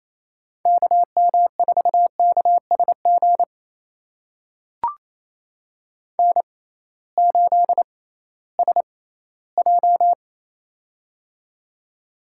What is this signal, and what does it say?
Morse code, heard as a keyed tone, about 28 wpm, spaced as for 16 wpm, 700 Hz. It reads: KM4KSG N8SJ
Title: USA Call Signs - 6x Character Spacing - 28wpm